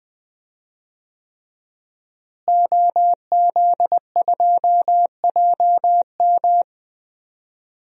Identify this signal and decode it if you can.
Morse code, heard as a keyed tone, about 20 wpm, 700 Hz. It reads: OZ2JM
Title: DXCC-100 Call Signs - 5 Characters - Repeated 3x - 20wpm